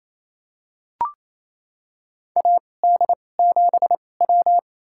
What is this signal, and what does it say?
Morse code, heard as a keyed tone, about 27 wpm, spaced as for 20 wpm, 700 Hz. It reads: AD7W
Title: USA Call Signs - 2x Character Spacing - 28wpm